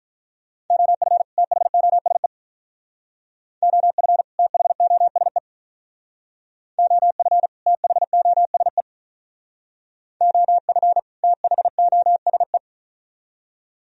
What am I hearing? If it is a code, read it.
Morse code, heard as a keyed tone, about 43 wpm, 700 Hz. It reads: OF THOSE OF THOSE OF THOSE OF THOSE
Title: Speed-Racing - Sets of 2 Words 35wpm